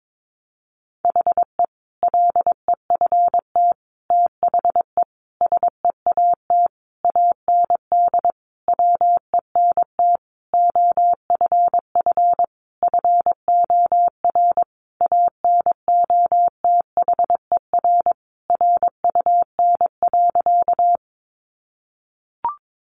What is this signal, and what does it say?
Morse code, heard as a keyed tone, about 22 wpm, 700 Hz. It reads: HE LEFT THE SEAT AND WENT OFF FOR ANOTHER RUN.
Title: Sentences from Top 1500 Words 22wpm